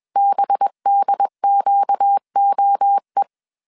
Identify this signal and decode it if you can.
Morse code, heard as a keyed tone, about 21 wpm, 800 Hz. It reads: BDQOE
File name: CW / DET / 1 6 1500C S B 20wpm 2048